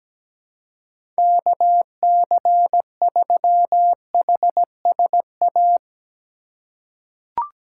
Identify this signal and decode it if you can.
Morse code, heard as a keyed tone, about 17 wpm, 700 Hz. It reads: KC3HSA
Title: USA Call Signs - 6 Characters - 2x Character Spacing - 17wpm